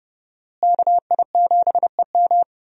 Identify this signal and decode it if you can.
Morse code, heard as a keyed tone, about 30 wpm, 700 Hz. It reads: KI7EM
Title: USA Call Signs - 5 Characters - Repeated 2x - 30wpm